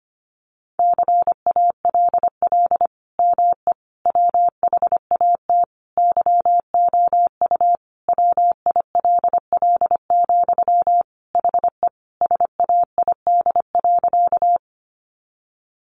Morse code, heard as a keyed tone, about 25 wpm, 700 Hz. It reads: CALL ME WHAT YOU WILL, HE SAID.
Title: Sentences from Top 200 Words 25wpm